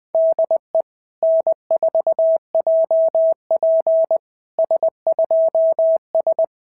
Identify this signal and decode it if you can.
Morse code, heard as a keyed tone, about 21 wpm, 650 Hz. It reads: DE N4JP S2S
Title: SOTA QSO Practice 20wpm